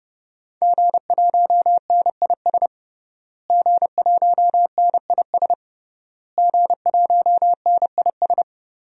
Morse code, heard as a keyed tone, about 30 wpm, 700 Hz. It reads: G1NIS G1NIS G1NIS
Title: DXCC-100 Call Signs - 5 Characters - Repeated 3x - 30wpm